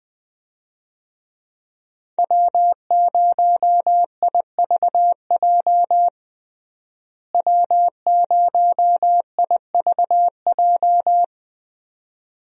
Morse code, heard as a keyed tone, about 20 wpm, 700 Hz. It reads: W0IVJ W0IVJ
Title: USA Call Signs - 5 Characters - Repeated 2x - 20wpm